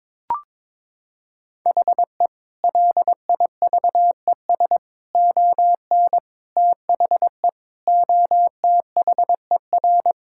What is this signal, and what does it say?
Morse code, heard as a keyed tone, about 22 wpm, 700 Hz. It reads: HE LIVES ON THE OTHER
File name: practice / 22wpm / Sentences with 4-Word Phrases - 22wpm